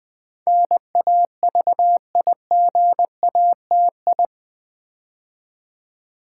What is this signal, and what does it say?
Morse code, heard as a keyed tone, about 20 wpm, 700 Hz. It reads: NAVIGATI
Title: Build A Word - 4 Letters At A Time - 10-Letter Words - 20wpm